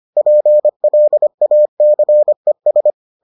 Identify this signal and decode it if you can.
Morse code, heard as a keyed tone, about 25 wpm, 600 Hz. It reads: PLACES